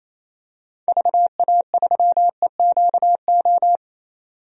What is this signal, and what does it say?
Morse code, heard as a keyed tone, about 28 wpm, 700 Hz. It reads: VA3EQO
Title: DXCC-100 Call Signs - 6 Characters - 28wpm